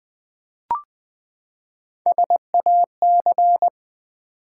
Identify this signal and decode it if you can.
Morse code, heard as a keyed tone, about 20 wpm, 700 Hz. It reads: SAC
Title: Build A Word - 3 Letters At A Time - 9-Letter Words - 20wpm